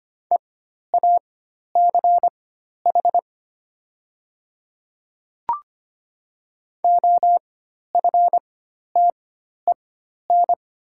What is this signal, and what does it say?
Morse code, heard as a keyed tone, about 25 wpm, spaced as for 12 wpm, 690 Hz. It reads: EACH OFTEN
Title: 4x Character Spacing - Top 300 Words - 25wpm